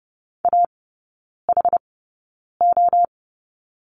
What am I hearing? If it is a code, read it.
Morse code, heard as a keyed tone, about 30 wpm, spaced as for 9 wpm, 700 Hz. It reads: AHO
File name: course / 30:10wpm / 087 Lesson 15e - Number 5 - Three Characters 30wpm